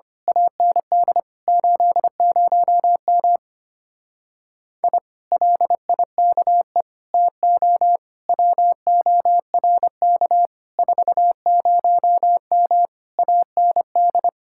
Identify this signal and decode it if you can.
Morse code, heard as a keyed tone, about 25 wpm, 700 Hz. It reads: AND 80M I LIKE TO WORK 40M AND
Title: Speed-Racing Common QSO Phrases Encore - 25wpm